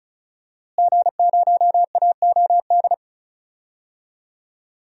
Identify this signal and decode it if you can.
Morse code, heard as a keyed tone, about 36 wpm, 700 Hz. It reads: G0AOD
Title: DXCC-100 Call Signs - 5 Characters - Repeated 2x - 35wpm